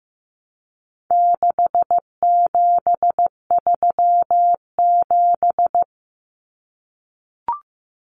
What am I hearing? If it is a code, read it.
Morse code, heard as a keyed tone, about 15 wpm, 700 Hz. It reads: 6737
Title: Common Cut Numbers - 4-Digit 15wpm